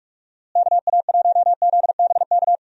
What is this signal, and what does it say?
Morse code, heard as a keyed tone, about 46 wpm, 700 Hz. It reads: KA1ZBK
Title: USA Call Signs - 6 Characters - Repeated 3x - 45wpm